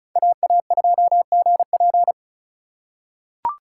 Morse code, heard as a keyed tone, about 36 wpm, 690 Hz. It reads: AA2GP
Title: USA Call Signs - 5 Characters - 3x Character Spacing - 35wpm